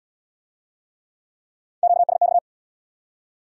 Morse code, heard as a keyed tone, about 74 wpm, 700 Hz. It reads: QSX